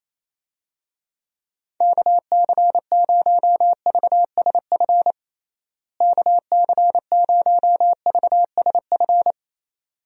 Morse code, heard as a keyed tone, about 27 wpm, 700 Hz. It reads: KC0VSF KC0VSF
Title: USA Call Signs - 6 Characters - Repeated 2x - 28wpm